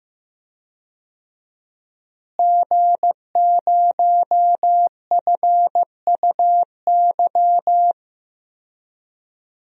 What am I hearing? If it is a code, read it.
Morse code, heard as a keyed tone, about 15 wpm, 700 Hz. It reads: G0FUY
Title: DXCC-100 Call Signs - 5 Characters - Repeated 2x - 15wpm